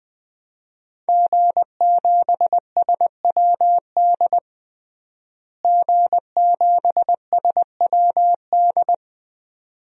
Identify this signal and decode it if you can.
Morse code, heard as a keyed tone, about 20 wpm, 700 Hz. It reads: G7SWD G7SWD
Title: DXCC-100 Call Signs - 5 Characters - Repeated 2x - 20wpm